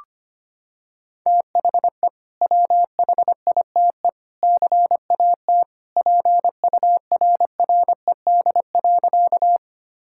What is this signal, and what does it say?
Morse code, heard as a keyed tone, about 25 wpm, 700 Hz. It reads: THE WHITE CAT PURRED.
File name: practice / 25wpm / Sentences with 2-Word Phrases - 25wpm